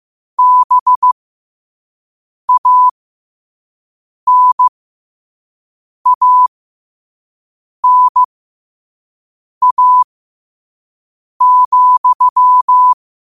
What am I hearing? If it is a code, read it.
Morse code, heard as a keyed tone, about 15 wpm, spaced as for 5 wpm, 1000 Hz. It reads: BANANA,
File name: qotdfiles / 5wpm